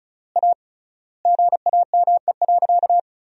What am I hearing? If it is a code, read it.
Morse code, heard as a keyed tone, about 34 wpm, 700 Hz. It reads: A GAME.
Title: 3x Word-Spacing - Sentences from Top 500 Words - 35wpm